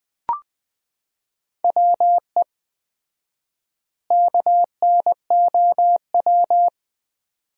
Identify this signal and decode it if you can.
Morse code, heard as a keyed tone, about 21 wpm, 700 Hz. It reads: WE KNOW
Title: 4x Word-Spacing - Sentences from Top 500 Words - 20wpm